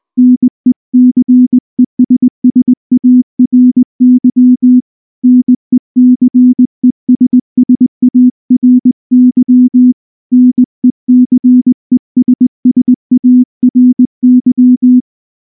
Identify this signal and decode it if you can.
Morse code, heard as a keyed tone, about 19 wpm, 250 Hz. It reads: NECESSARY NECESSARY NECESSARY
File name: morse / morse-training-20WPM